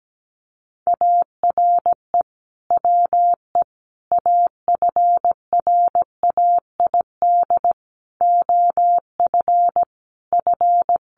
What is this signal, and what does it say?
Morse code, heard as a keyed tone, about 17 wpm, 700 Hz. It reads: ARE WE AFRAID OF F
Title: Sentences from Top 1500 Words 17wpm